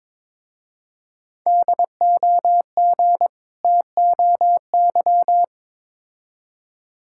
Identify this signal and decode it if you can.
Morse code, heard as a keyed tone, about 22 wpm, 700 Hz. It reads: DOG TOY
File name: Difficult Words with Hints - 22wpm